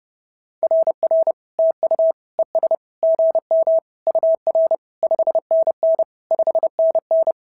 Morse code, heard as a keyed tone, about 30 wpm, 650 Hz. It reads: RR TU ES GM UR 5NN 5NN
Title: SOTA QSO Practice 30wpm